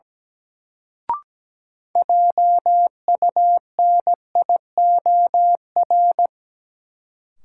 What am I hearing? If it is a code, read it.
Morse code, heard as a keyed tone, about 17 wpm, 700 Hz. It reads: JUNIOR
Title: Sentences from Top 1000 Words - Review Part 7 - New Word Drill - 17wpm